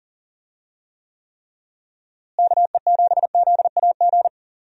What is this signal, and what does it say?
Morse code, heard as a keyed tone, about 40 wpm, 700 Hz. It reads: KE7ZAG
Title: USA Call Signs - 6 Characters - Repeated 3x - 40wpm